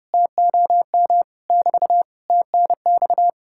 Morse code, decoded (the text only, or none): TOM <BT> TNX